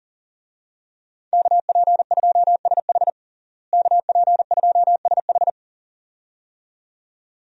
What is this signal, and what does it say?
Morse code, heard as a keyed tone, about 37 wpm, 700 Hz. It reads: KP2SH KP2SH
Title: DXCC-100 Call Signs - 5 Characters - Repeated 2x - 40wpm